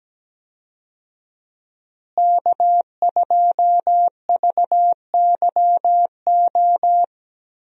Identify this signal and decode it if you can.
Morse code, heard as a keyed tone, about 17 wpm, 700 Hz. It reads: K2VYO
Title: USA Call Signs - 5 Characters - Repeated 3x - 17wpm